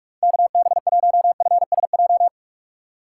Morse code, heard as a keyed tone, about 46 wpm, 700 Hz. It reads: KB1FSJ
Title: USA Call Signs - 6 Characters - 2x Character Spacing - 45wpm